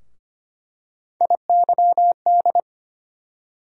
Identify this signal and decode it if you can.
Morse code, heard as a keyed tone, about 25 wpm, 700 Hz. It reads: IYD
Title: USA Call Sign Postfixes - 3 Letters 25wpm